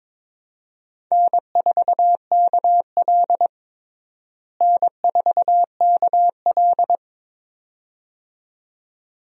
Morse code, heard as a keyed tone, about 22 wpm, 700 Hz. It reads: N4KL N4KL
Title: USA Call Signs - 4 Characters - Repeated 2x - 22wpm